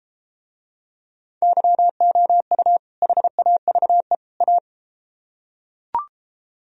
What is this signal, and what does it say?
Morse code, heard as a keyed tone, about 33 wpm, 700 Hz. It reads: YOU HAVE A